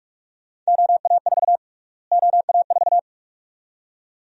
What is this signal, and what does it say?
Morse code, heard as a keyed tone, about 46 wpm, 700 Hz. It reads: OA4 OA4
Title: USA Call Sign Prefixes - Repeated 2x - 45wpm